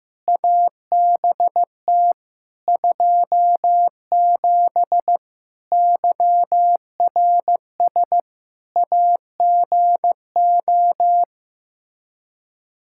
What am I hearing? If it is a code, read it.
Morse code, heard as a keyed tone, about 15 wpm, 700 Hz. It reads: ABT 27 YRS AGO
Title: Practice QSO - Exchange 3 - More Info - 15wpm